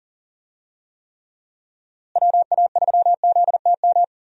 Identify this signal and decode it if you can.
Morse code, heard as a keyed tone, about 40 wpm, 700 Hz. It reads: WA3ZTM
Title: USA Call Signs - 6 Characters - Repeated 2x - 40wpm